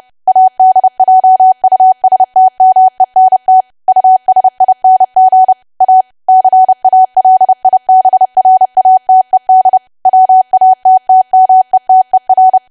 Morse code, heard as a keyed tone, about 30 wpm, 750 Hz. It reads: ADJUSTMENT USING A CALIBRATED WATTMETER
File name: cwa / W1AW-Level3 / 30WPM